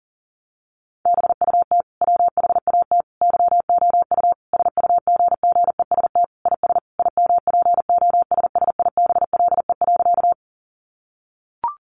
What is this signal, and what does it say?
Morse code, heard as a keyed tone, about 40 wpm, 700 Hz. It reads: BUT WHAT YOU SUGGEST IS IMPOSSIBLE.
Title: Sentences from Top 1300 Words 40wpm